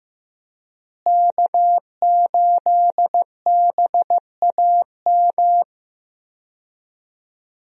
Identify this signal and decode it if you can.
Morse code, heard as a keyed tone, about 15 wpm, 700 Hz. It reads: K8BAM
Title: USA Call Signs - 5 Characters - Repeated 2x - 15wpm